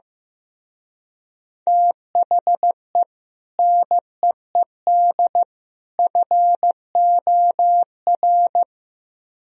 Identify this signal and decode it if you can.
Morse code, heard as a keyed tone, about 16 wpm, 700 Hz. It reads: THE NEED FOR
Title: Speed-Racing - Sets of 3 Words 15wpm